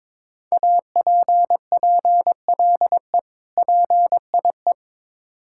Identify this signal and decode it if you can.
Morse code, heard as a keyed tone, about 22 wpm, 700 Hz. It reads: APPLE PIE
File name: Difficult Words with Hints - 22wpm